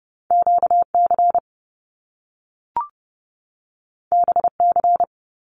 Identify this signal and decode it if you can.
Morse code, heard as a keyed tone, about 30 wpm, 700 Hz. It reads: QC BC
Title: Canadian Provinces and Territories 30wpm